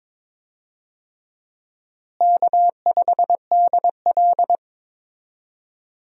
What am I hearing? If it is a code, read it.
Morse code, heard as a keyed tone, about 22 wpm, 700 Hz. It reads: K5DL
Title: USA Call Signs - 4 Characters - Repeated 3x - 22wpm